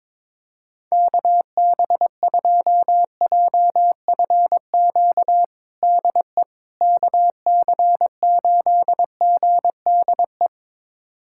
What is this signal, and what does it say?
Morse code, heard as a keyed tone, about 22 wpm, 700 Hz. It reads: KB2JFQ DE KC8GDE
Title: USA Callsign DE Callsign - 6 Characters - 22wpm